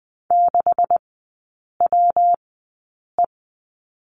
Morse code, heard as a keyed tone, about 20 wpm, spaced as for 8 wpm, 700 Hz. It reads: 6WE
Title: 152 Lesson 25e - Letter p - Three Characters 20wpm